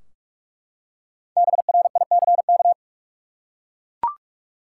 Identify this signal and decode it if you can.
Morse code, heard as a keyed tone, about 44 wpm, 700 Hz. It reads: BRICK
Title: Five Letter Words - 45wpm